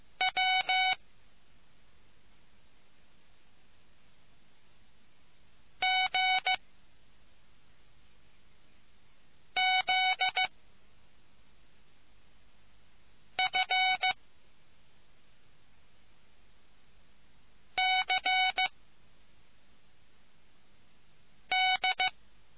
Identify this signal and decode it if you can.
Morse code, heard as a keyed tone, about 15 wpm, 750 Hz. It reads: W G Z F C D